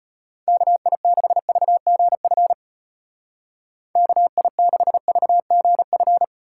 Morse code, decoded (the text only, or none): KI6VGF KI6VGF